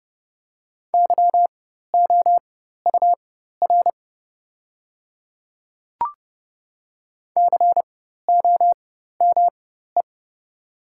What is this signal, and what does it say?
Morse code, heard as a keyed tone, about 30 wpm, spaced as for 14 wpm, 690 Hz. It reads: YOUR COME
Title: 4x Character Spacing - Top 200 Words - 30wpm